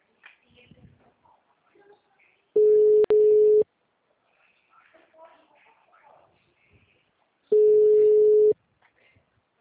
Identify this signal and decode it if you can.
Ringback tone (Europe, 425 Hz), with speech.